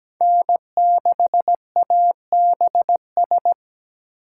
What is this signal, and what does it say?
Morse code, heard as a keyed tone, about 20 wpm, 700 Hz. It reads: N6ABS
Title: Speed Racing - USA 1x3 Call Signs 17wpm